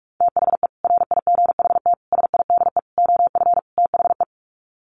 Morse code, heard as a keyed tone, about 44 wpm, 700 Hz. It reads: THE RIGHT SIDE OF THE
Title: Sets of 5 Words Encore 45wpm